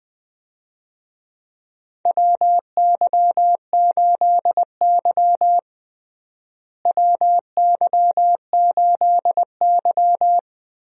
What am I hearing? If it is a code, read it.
Morse code, heard as a keyed tone, about 20 wpm, 700 Hz. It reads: WY8Y WY8Y
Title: USA Call Signs - 4 Characters - Repeated 2x - 20wpm